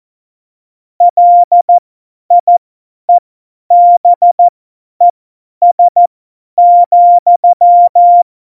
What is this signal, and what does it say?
Morse code, heard as a keyed tone, about 14 wpm, spaced as for 10 wpm, 700 Hz. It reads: LIEBES,
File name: Morsekurs / Witze / Witz 27. August 2019, 50 F70